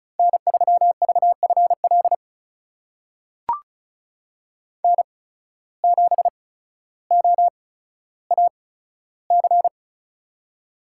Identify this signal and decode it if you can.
Morse code, heard as a keyed tone, about 36 wpm, spaced as for 15 wpm, 700 Hz. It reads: N3VFL N7OAC
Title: USA Call Signs - 5 Characters - 8x Character Spacing - 35wpm